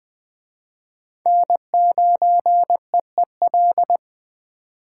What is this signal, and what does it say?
Morse code, heard as a keyed tone, about 20 wpm, 700 Hz. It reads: N9EEL
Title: Speed Racing - USA 1x3 Call Signs 17wpm